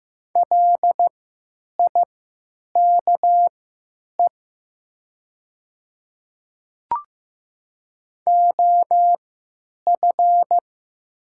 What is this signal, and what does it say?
Morse code, heard as a keyed tone, about 15 wpm, spaced as for 9 wpm, 700 Hz. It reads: LIKE OF
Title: 3x Character Spacing - Top 200 Words - 15wpm